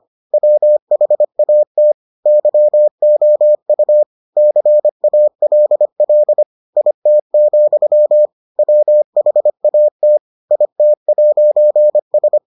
Morse code, heard as a keyed tone, about 25 wpm, 600 Hz. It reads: WHAT YOU CALL IT, WHAT IT'S